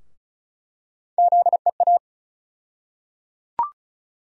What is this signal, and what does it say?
Morse code, heard as a keyed tone, about 34 wpm, 700 Hz. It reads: ZEA
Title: USA Call Sign Postfixes - 3 Letters 35wpm